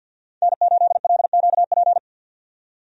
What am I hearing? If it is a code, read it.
Morse code, heard as a keyed tone, about 50 wpm, 700 Hz. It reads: N8LQP
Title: USA Call Signs - 5 Characters - 50wpm